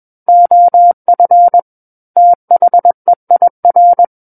Morse code, heard as a keyed tone, about 21 wpm, 700 Hz. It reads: OF THEIR
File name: Book One / 12-Chapter 12 - 21 WPM